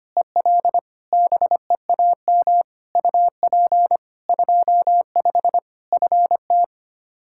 Morse code, heard as a keyed tone, about 25 wpm, 700 Hz. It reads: EL BEAM UP 25 FT
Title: Practice QSO - Exchange 2 - Info - 25wpm